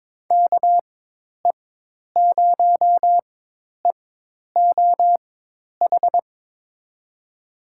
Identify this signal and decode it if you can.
Morse code, heard as a keyed tone, about 22 wpm, spaced as for 10 wpm, 700 Hz. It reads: KE0EOH
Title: USA Call Signs - 6 Characters - 4x Character Spacing - 22wpm